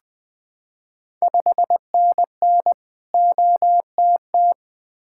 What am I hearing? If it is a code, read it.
Morse code, heard as a keyed tone, about 20 wpm, 700 Hz. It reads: 5NN OTT
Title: Ontario QSO Party Exchange 20wpm